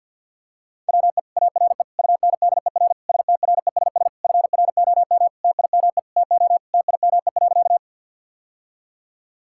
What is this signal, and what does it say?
Morse code, heard as a keyed tone, about 50 wpm, 700 Hz. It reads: WE ARE UNDER STRESS FROM TIME TO TIME.
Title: Sentences from Top 1000 Words - Review Part 6 - 50wpm